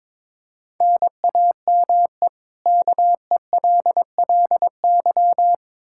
Morse code, heard as a keyed tone, about 22 wpm, 700 Hz. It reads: NAME KELLY
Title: Practice QSO - Exchange 1 - Beginning - 22wpm